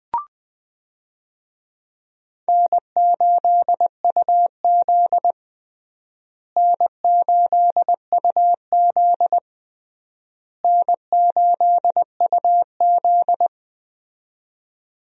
Morse code, heard as a keyed tone, about 20 wpm, 700 Hz. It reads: N8UZ N8UZ N8UZ
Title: USA Call Signs - 4 Characters - Repeated 3x - 20wpm